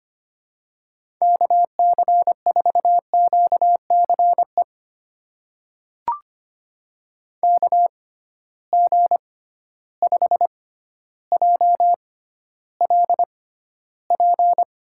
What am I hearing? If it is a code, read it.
Morse code, heard as a keyed tone, about 25 wpm, spaced as for 13 wpm, 700 Hz. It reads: KC4QCE KG5JLP
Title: USA Call Signs - 6 Characters - 6x Character Spacing - 25wpm